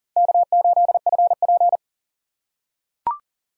Morse code, heard as a keyed tone, about 40 wpm, 700 Hz. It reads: K8FP
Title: USA Call Signs - 4 Characters - 4x Character Spacing - 40wpm